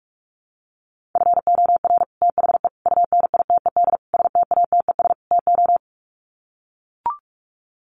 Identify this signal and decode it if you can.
Morse code, heard as a keyed tone, about 44 wpm, 700 Hz. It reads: FOR THE UNITED STATES TO
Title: Sets of 5 Words Encore 45wpm